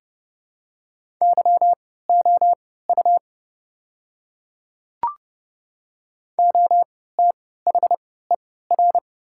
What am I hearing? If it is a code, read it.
Morse code, heard as a keyed tone, about 30 wpm, spaced as for 17 wpm, 690 Hz. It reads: YOU OTHER
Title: 3x Character Spacing - Top 100 Words - 30wpm